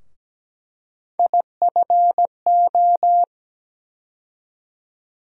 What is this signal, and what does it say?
Morse code, heard as a keyed tone, about 17 wpm, 700 Hz. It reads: IFO
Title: USA Call Sign Postfixes - 3 Letters 17wpm